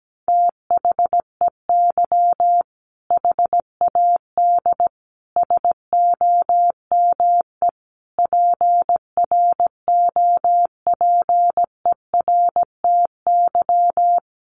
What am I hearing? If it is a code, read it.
Morse code, heard as a keyed tone, about 17 wpm, 700 Hz. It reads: THEY HAD SOME PROPERTY
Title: Sentences from Top 1500 Words 17wpm